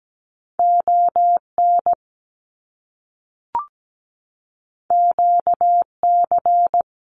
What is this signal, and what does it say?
Morse code, heard as a keyed tone, about 17 wpm, 700 Hz. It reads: ON QC
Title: Canadian Provinces and Territories 17wpm